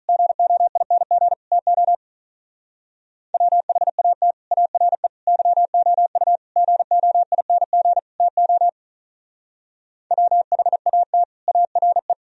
Morse code, heard as a keyed tone, about 43 wpm, 700 Hz. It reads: GOING TO WHAT ARE YOU GOING TO WHAT ARE
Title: Speed-Racing - Sets of 5 Words 35wpm